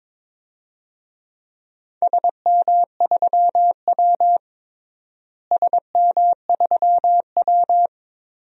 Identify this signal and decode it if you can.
Morse code, heard as a keyed tone, about 22 wpm, 700 Hz. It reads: SM3W SM3W
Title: DXCC-100 Call Signs - 4 Characters - Repeated 2x - 22wpm